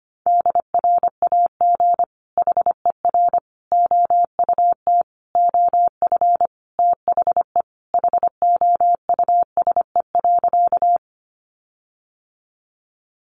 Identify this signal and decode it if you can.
Morse code, heard as a keyed tone, about 25 wpm, 700 Hz. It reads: DRAG HER OUT OF THE HOUSE.
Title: Sentences from Top 1500 Words 25wpm